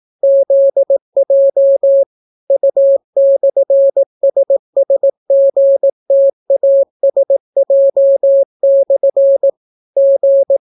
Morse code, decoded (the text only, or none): ZJ U/SSGTASJ/ G